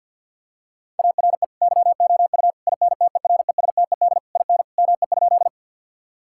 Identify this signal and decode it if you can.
Morse code, heard as a keyed tone, about 50 wpm, 700 Hz. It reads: ARE YOU INTERESTED IN ME?
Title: Sentences from Top 1000 Words - Review Part 6 - 50wpm